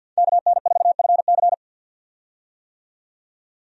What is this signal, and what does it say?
Morse code, heard as a keyed tone, about 50 wpm, 700 Hz. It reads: KN4FC
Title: USA Call Signs - 5 Characters - 50wpm